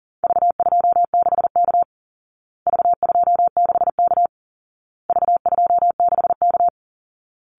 Morse code, heard as a keyed tone, about 40 wpm, 700 Hz. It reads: V26K V26K V26K
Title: Call Signs Repeated 2x - 40wpm